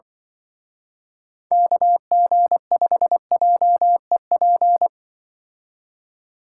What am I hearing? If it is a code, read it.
Morse code, heard as a keyed tone, about 24 wpm, 700 Hz. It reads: KG5JEP
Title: Speed Racing - USA Call Signs 20wpm